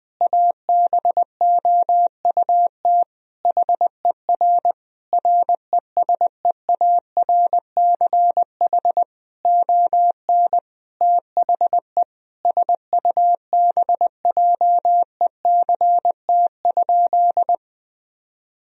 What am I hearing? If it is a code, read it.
Morse code, heard as a keyed tone, about 19 wpm, 700 Hz. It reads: ABOUT HER RESEARCH ON THE SUBJECT?
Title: Sentences from Top 1000 Words - Review Part 4 - 20wpm